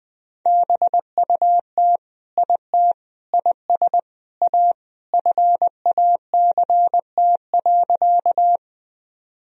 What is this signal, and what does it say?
Morse code, heard as a keyed tone, about 21 wpm, 700 Hz. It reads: BUT IT IS A FACT.
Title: Sentences from Top 1000 Words - Review Part 3 - 20wpm